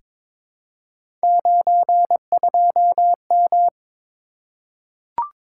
Morse code, heard as a keyed tone, about 22 wpm, 700 Hz. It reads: 92M
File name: practice / 22wpm / Three Letter-Number Combinations - 22wpm